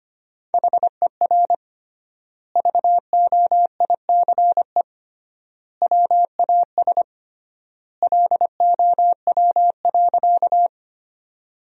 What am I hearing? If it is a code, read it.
Morse code, heard as a keyed tone, about 25 wpm, 700 Hz. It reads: HER VOICE WAS LOW.
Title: 3x Word-Spacing - Sentences from Top 500 Words - 25wpm